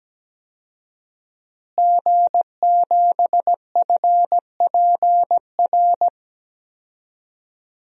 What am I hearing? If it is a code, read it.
Morse code, heard as a keyed tone, about 17 wpm, 700 Hz. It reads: G7FPR
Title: DXCC-100 Call Signs - 5 Characters - Repeated 2x - 17wpm